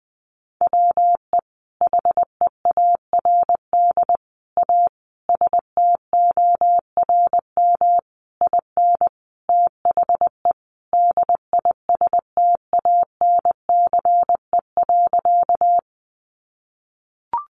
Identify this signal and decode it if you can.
Morse code, heard as a keyed tone, about 20 wpm, 700 Hz. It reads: WE HEARD A STORM IN THE DISTANCE.
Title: Sentences from Top 1500 Words 20wpm